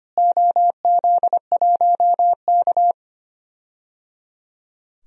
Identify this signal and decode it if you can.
Morse code, heard as a keyed tone, about 25 wpm, 700 Hz. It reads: OZ1K
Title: DXCC-100 Call Signs - 4 Characters - Repeated 3x - 25wpm